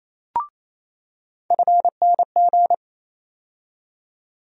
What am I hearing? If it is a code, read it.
Morse code, heard as a keyed tone, about 28 wpm, 700 Hz. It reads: FNG
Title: USA Call Sign Postfixes - 3 Letters 28wpm